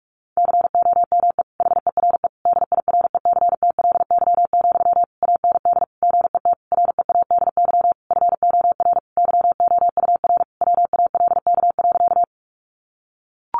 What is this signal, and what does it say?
Morse code, heard as a keyed tone, about 44 wpm, 700 Hz. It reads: COME HERE DIRECTLY, AND GET READY FOR YOUR WALK.
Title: Sentences from Top 800 Words 45wpm